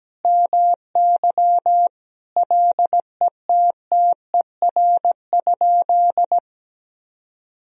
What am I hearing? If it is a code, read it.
Morse code, heard as a keyed tone, about 17 wpm, 700 Hz. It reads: MY LETTER?
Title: Sentences from Top 1000 Words - Review Part 3 - 17wpm